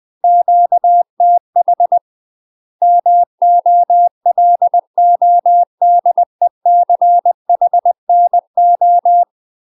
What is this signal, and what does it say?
Morse code, heard as a keyed tone, about 20 wpm, 700 Hz. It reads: QTH MOLODECHNO
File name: UK MASTER QSOs TRAINING / 20WPM / UK MASTER QSOs 0024